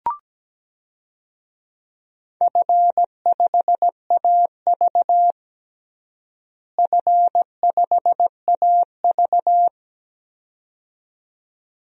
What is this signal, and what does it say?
Morse code, heard as a keyed tone, about 17 wpm, 700 Hz. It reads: F5AV F5AV
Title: DXCC-100 Call Signs - 4 Characters - Repeated 2x - 17wpm